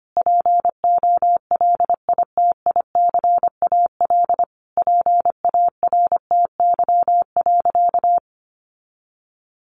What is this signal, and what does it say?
Morse code, heard as a keyed tone, about 25 wpm, 700 Hz. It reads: POLITICAL PARTY.
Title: Sentences from Top 1400 Words 25wpm